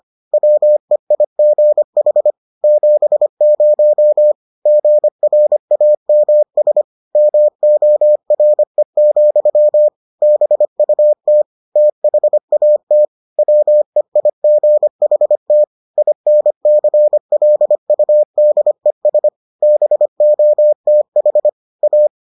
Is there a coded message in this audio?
Yes: Morse code from 0.3 to 22.1 s, heard as a keyed tone, about 25 wpm, 600 Hz. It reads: WEIGH 70 GRAMS MORE, BUT THAT WEIGHT INCLUDES BOTH A